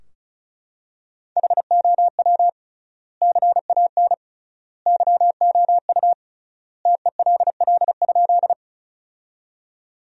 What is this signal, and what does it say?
Morse code, heard as a keyed tone, about 34 wpm, 700 Hz. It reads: HOW CAN YOU TELL?
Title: 3x Word-Spacing - Sentences from Top 200 Words - 35wpm